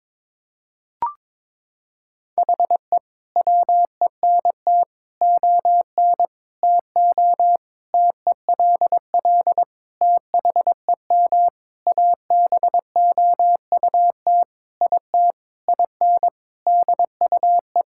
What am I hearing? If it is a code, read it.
Morse code, heard as a keyed tone, about 22 wpm, 700 Hz. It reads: HE WENT ON TO TELL THEM ABOUT IT IN DUE
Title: Sentences from Top 1000 Words - Review Part 6 - 22wpm